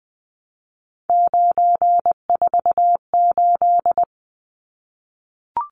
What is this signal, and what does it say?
Morse code, heard as a keyed tone, about 20 wpm, 700 Hz. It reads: 948
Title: Numbers - 3-Digit 20wpm